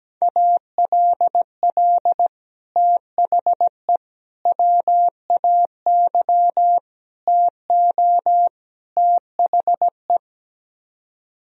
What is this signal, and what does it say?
Morse code, heard as a keyed tone, about 17 wpm, 700 Hz. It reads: ALL THE WAY TO THE
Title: Speed-Racing - Sets of 5 Words 17wpm